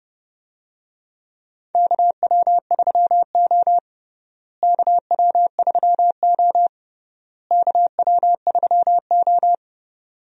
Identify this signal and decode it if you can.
Morse code, heard as a keyed tone, about 30 wpm, 700 Hz. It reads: KW3O KW3O KW3O
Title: USA Call Signs - 4 Characters - Repeated 3x - 30wpm